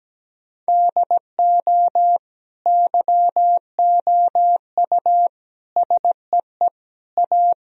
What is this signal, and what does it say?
Morse code, heard as a keyed tone, about 17 wpm, 700 Hz. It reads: DO YOU SEE A
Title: Sentences from Top 1000 Words - Review Part 4 - 17wpm